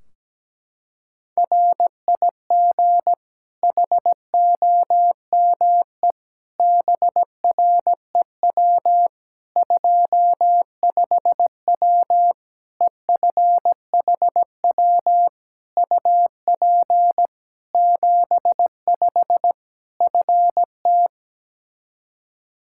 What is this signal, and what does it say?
Morse code, heard as a keyed tone, about 17 wpm, 700 Hz. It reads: RIG HOME BREW 25W EFHW UP 75 FT
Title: K1USN SST QSO Practice 17wpm